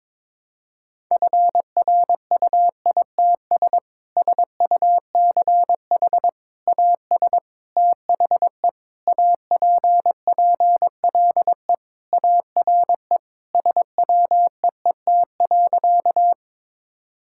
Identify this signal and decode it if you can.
Morse code, heard as a keyed tone, about 22 wpm, 700 Hz. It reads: FRUITS SUCH AS THE APPLE ARE SWEET.
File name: practice / 22wpm / Sentences with 3-Word Phrases - 22wpm